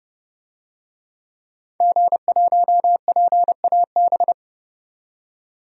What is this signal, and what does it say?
Morse code, heard as a keyed tone, about 29 wpm, 700 Hz. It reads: G1PAB
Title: DXCC-100 Call Signs - 5 Characters - 30wpm